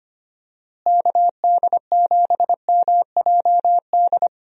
Morse code, heard as a keyed tone, about 25 wpm, 700 Hz. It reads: KD7MJD
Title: USA Call Signs - 6 Characters - Repeated 3x - 25wpm